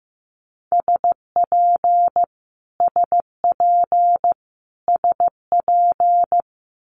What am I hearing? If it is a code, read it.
Morse code, heard as a keyed tone, about 15 wpm, 700 Hz. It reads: SP SP SP
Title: Multi-letter Phonemes Repeated 3x 15wpm